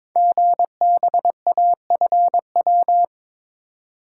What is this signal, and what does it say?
Morse code, heard as a keyed tone, about 23 wpm, 700 Hz. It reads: GBAFW
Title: Code Groups - 5 Letters - 22wpm